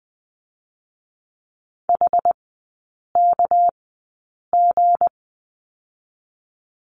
Morse code, heard as a keyed tone, about 20 wpm, 700 Hz. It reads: H K G
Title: 198 Lesson 32e - Letter k - Three Characters 20wpm